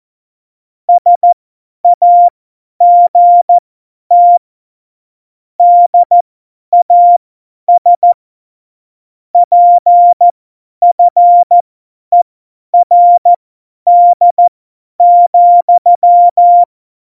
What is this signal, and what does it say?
Morse code, heard as a keyed tone, about 14 wpm, spaced as for 10 wpm, 700 Hz. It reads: SAGT DAS PFERD,
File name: Morsekurs / Witze / Witz 27. November 2018, 50 F70